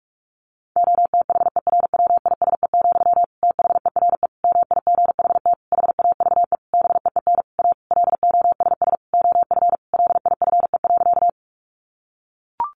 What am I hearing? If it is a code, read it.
Morse code, heard as a keyed tone, about 44 wpm, 700 Hz. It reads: OTHERWISE, THERE MIGHT HAVE BEEN A LOSS OF LIFE.
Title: Sentences from Top 1400 Words 45wpm